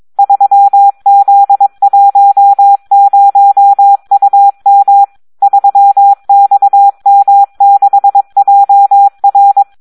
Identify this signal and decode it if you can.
Morse code, heard as a keyed tone, about 22 wpm, 800 Hz. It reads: 3Z10UM 3XM6JR
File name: Audio practica CW / SD 1 22